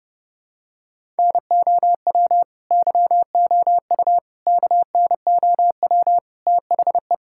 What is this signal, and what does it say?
Morse code, heard as a keyed tone, about 32 wpm, 700 Hz. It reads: NOW YOU KNOW THE
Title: Sentences from Top 1000 Words - Review Part 6 - 30wpm